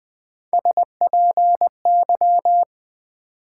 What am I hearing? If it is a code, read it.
Morse code, heard as a keyed tone, about 20 wpm, 700 Hz. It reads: SPY
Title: Build A Word - 1 Letter At A Time - 3-Letter Words - 20wpm